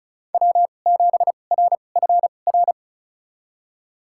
Morse code, heard as a keyed tone, about 36 wpm, spaced as for 25 wpm, 700 Hz. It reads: W7RFR
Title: USA Call Signs - 5 Characters - 2x Character Spacing - 35wpm